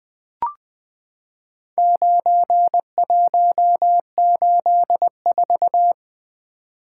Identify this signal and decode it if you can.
Morse code, heard as a keyed tone, about 21 wpm, 700 Hz. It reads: 9184
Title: Four Letter-Number Combinations - 20wpm